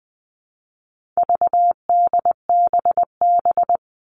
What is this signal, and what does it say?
Morse code, heard as a keyed tone, about 20 wpm, 700 Hz. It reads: VDBB
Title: All Cut Numbers - 4-Digit 20wpm